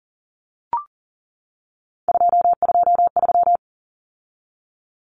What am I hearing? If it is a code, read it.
Morse code, heard as a keyed tone, about 40 wpm, 700 Hz. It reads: 223
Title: Common Cut Numbers - 3-Digit 40wpm